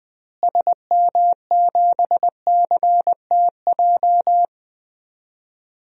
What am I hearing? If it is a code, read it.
Morse code, heard as a keyed tone, about 20 wpm, 700 Hz. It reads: SM7CTJ
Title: DXCC-100 Call Signs - 6 Characters - Repeated 2x - 20wpm